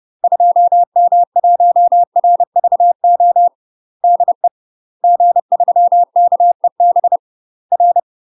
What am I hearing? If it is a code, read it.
Morse code, heard as a keyed tone, about 30 wpm, 700 Hz. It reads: 2M1RVO DE G3KEB R
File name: UK MASTER QSOs TRAINING / 30WPM / UK MASTER QSOs 0011